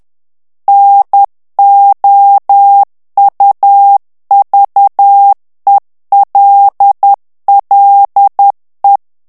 Morse code, heard as a keyed tone, about 10 wpm, 800 Hz. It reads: NOUVELLE
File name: morse mp3 / FRANCAIS TEXTE CLAIR 2 VITESSE 10 MOTS MINUTES